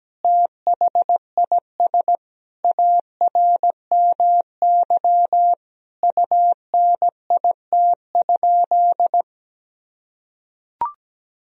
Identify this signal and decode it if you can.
Morse code, heard as a keyed tone, about 17 wpm, 700 Hz. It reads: THIS ARMY UNIT?
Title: Sentences from Top 1000 Words - Review Part 2 - 17wpm